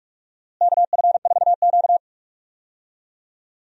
Morse code, heard as a keyed tone, about 41 wpm, 700 Hz. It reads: KF4Q
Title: USA Call Signs - 4 Characters - Repeated 2x - 45wpm